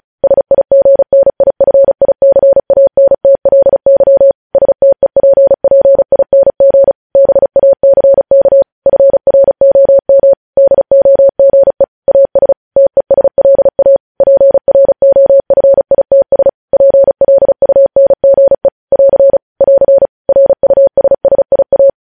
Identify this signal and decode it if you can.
Morse code, heard as a keyed tone, about 36 wpm, 550 Hz. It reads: SIGNIFICANTLY STEPPING BACK FROM DOGE AS TESLA PROFITS PLUNGE <AR> <AR> RUSSIA
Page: News Headlines in Morse Code at 35 WPM